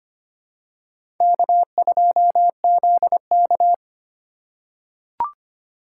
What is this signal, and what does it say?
Morse code, heard as a keyed tone, about 25 wpm, 700 Hz. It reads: K2ZK
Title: USA Call Signs - 4 Characters - 3x Character Spacing - 25wpm